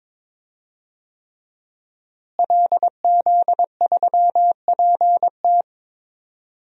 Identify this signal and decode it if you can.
Morse code, heard as a keyed tone, about 21 wpm, 700 Hz. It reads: LZ3PT
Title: DXCC-100 Call Signs - 5 Characters - Repeated 3x - 22wpm